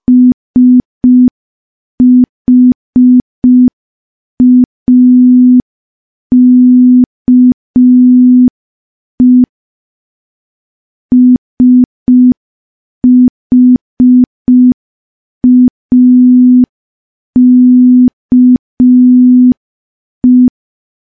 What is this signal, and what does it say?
Morse code, heard as a keyed tone, about 5 wpm, 250 Hz. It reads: SHAKE SHAKE
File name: morse / morse-training-05WPM